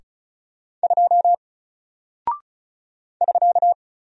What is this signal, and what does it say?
Morse code, heard as a keyed tone, about 36 wpm, 700 Hz. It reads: 2 <SK>
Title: Single Character (Letter:Number:Punctuation:Prosign) Alt 35wpm